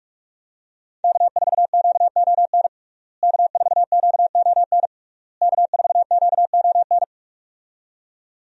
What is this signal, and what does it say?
Morse code, heard as a keyed tone, about 46 wpm, 700 Hz. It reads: K4QON K4QON K4QON
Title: USA Call Signs - 5 Characters - Repeated 3x - 45wpm